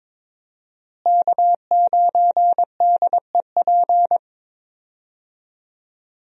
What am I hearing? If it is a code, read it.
Morse code, heard as a keyed tone, about 22 wpm, 700 Hz. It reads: K9DEP
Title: USA Call Signs - 5 Characters - Repeated 3x - 22wpm